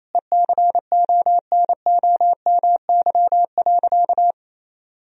Morse code, heard as a keyed tone, about 28 wpm, 700 Hz. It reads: ECONOMY.
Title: Sentences from Top 1000 Words - Review Part 3 - 28wpm